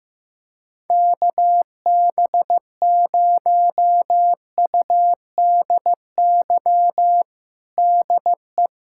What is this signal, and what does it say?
Morse code, heard as a keyed tone, about 15 wpm, 700 Hz. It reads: KB0UDY DE
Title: USA Callsign DE Callsign - 6 Characters - 15wpm